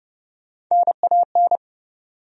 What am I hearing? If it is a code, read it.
Morse code, heard as a keyed tone, about 30 wpm, 700 Hz. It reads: NAN
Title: USA Common Names - Max 3 Letters - 30wpm